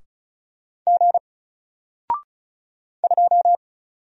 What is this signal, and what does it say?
Morse code, heard as a keyed tone, about 33 wpm, 700 Hz. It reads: G 2
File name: Single-Letter-Number-35wpm